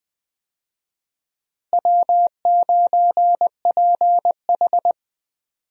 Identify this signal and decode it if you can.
Morse code, heard as a keyed tone, about 21 wpm, 700 Hz. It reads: W9PH
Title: USA Call Signs - 4 Characters - Repeated 3x - 20wpm